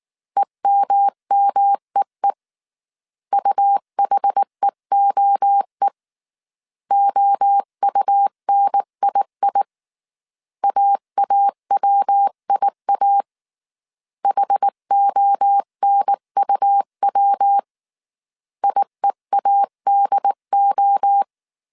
Morse code, heard as a keyed tone, about 19 wpm, 800 Hz. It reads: EMMEE UHEOE OUNII AAWIA HONUW IEADO